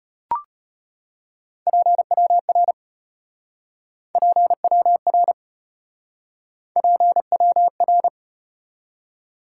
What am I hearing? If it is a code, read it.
Morse code, heard as a keyed tone, about 34 wpm, 700 Hz. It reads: PWR PWR PWR